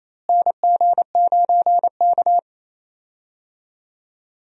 Morse code, heard as a keyed tone, about 28 wpm, 700 Hz. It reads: NG9K
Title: USA Call Signs - 4 Characters - Repeated 2x - 28wpm